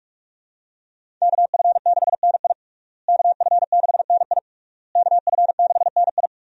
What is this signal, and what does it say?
Morse code, heard as a keyed tone, about 42 wpm, 700 Hz. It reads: KF6NI KF6NI KF6NI
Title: USA Call Signs - 5 Characters - Repeated 3x - 45wpm